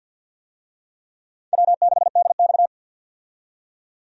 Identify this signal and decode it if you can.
Morse code, heard as a keyed tone, about 50 wpm, 700 Hz. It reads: W6DX
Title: USA Call Signs - 4 Characters - 50wpm